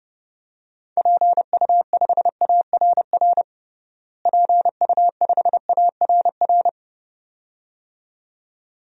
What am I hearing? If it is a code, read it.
Morse code, heard as a keyed tone, about 30 wpm, 700 Hz. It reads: PU5ARR PU5ARR
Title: DXCC-100 Call Signs - 6 Characters - Repeated 2x - 30wpm